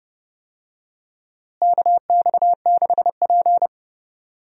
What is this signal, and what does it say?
Morse code, heard as a keyed tone, about 30 wpm, 700 Hz. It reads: KX6P
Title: USA Call Signs - 4 Characters - Repeated 3x - 30wpm